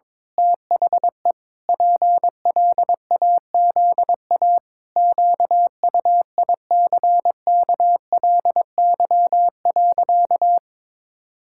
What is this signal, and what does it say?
Morse code, heard as a keyed tone, about 22 wpm, 700 Hz. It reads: THE PLAZA QUICKLY.
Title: Pangram Challenge v2 - 22wpm